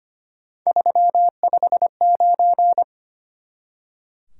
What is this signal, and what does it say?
Morse code, heard as a keyed tone, about 32 wpm, 700 Hz. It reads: 359